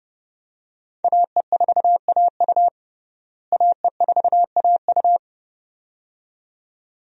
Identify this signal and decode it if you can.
Morse code, heard as a keyed tone, about 30 wpm, 700 Hz. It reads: AE4AU AE4AU
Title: USA Call Signs - 5 Characters - Repeated 2x - 30wpm